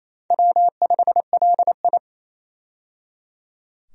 Morse code, heard as a keyed tone, about 27 wpm, 700 Hz. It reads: W5LI
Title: USA Call Signs - 4 Characters - 28wpm